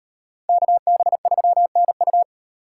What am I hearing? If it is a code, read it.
Morse code, heard as a keyed tone, about 38 wpm, 700 Hz. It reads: KB3NU